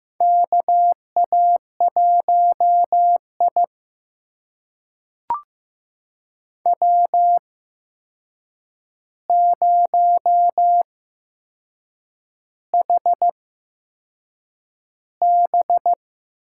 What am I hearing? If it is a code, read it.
Morse code, heard as a keyed tone, about 15 wpm, spaced as for 6 wpm, 700 Hz. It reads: KA1I W0HB
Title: USA Call Signs - 8x Character Spacing - 15wpm